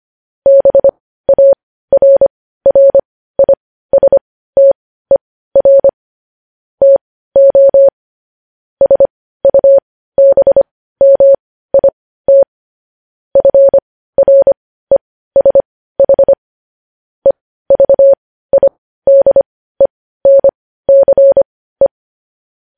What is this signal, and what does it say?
Morse code, heard as a keyed tone, about 25 wpm, spaced as for 15 wpm, 550 Hz. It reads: BARRISTER TO SUBMIT FRESH EVIDENCE
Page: news headlines in morse code at 15 wpm